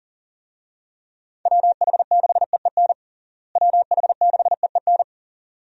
Morse code, heard as a keyed tone, about 40 wpm, 700 Hz. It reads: WH6EEN WH6EEN
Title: USA Call Signs - 6 Characters - Repeated 2x - 40wpm